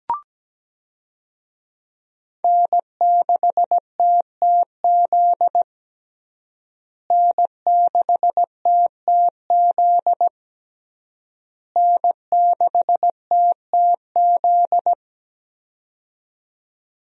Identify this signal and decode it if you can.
Morse code, heard as a keyed tone, about 17 wpm, 700 Hz. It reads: N6TTZ N6TTZ N6TTZ
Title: USA Call Signs - 5 Characters - Repeated 3x - 17wpm